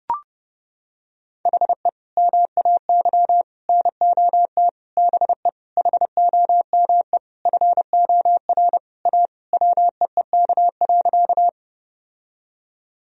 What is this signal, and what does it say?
Morse code, heard as a keyed tone, about 30 wpm, 700 Hz. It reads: HE MAY NOT BE HOME FOR A WEEK.
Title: Sentences from Top 1000 Words - Review Part 2 - 30wpm